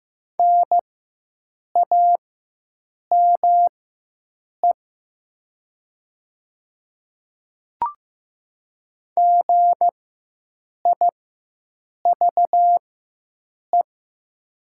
Morse code, heard as a keyed tone, about 15 wpm, spaced as for 7 wpm, 700 Hz. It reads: NAME GIVE
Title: 4x Character Spacing - Top 200 Words - 15wpm